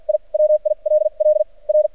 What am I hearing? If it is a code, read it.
Morse code, heard as a keyed tone, about 48 wpm, spaced as for 39 wpm, 610 Hz. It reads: I WILL R
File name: archived / CRSnet